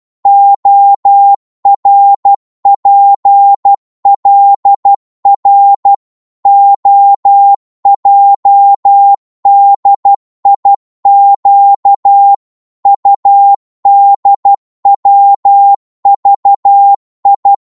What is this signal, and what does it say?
Morse code, heard as a keyed tone, about 12 wpm, 800 Hz. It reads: ORPLR OJDIQ UDWVI
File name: Morse / ltr-12wpm